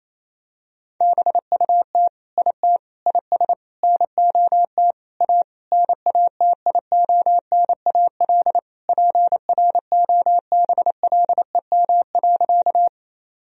Morse code, heard as a keyed tone, about 28 wpm, 700 Hz. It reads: BUT IT IS NOT A NATIONAL PROBLEM.
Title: Sentences from Top 1000 Words - Review Part 4 - 28wpm